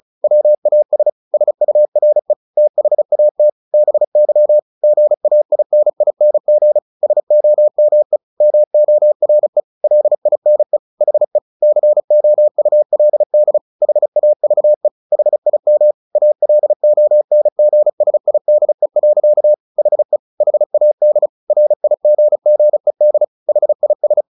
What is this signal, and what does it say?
Morse code, heard as a keyed tone, about 34 wpm, 600 Hz. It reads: WAS SURE THAT BY GAINING SOME MORE LINE HE COULD HAVE HIM ALONGSIDE. HE HAD RIGGED HIS
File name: the old man and the sea 2 35wpm